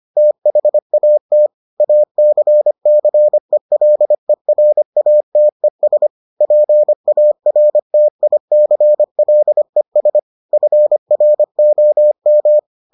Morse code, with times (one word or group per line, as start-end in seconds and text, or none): 0.2-1.5 THAT
1.8-6.1 ACCELERATES
6.4-10.2 PARTICLES
10.5-12.6 FROM